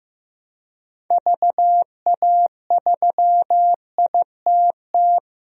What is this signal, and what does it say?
Morse code, heard as a keyed tone, about 15 wpm, 700 Hz. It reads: VA3ITT
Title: DXCC-100 Call Signs - 6 Characters - 15wpm